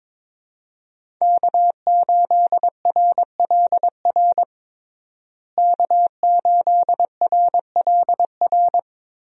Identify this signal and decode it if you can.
Morse code, heard as a keyed tone, about 22 wpm, 700 Hz. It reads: K8RLR K8RLR
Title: USA Call Signs - 5 Characters - Repeated 3x - 22wpm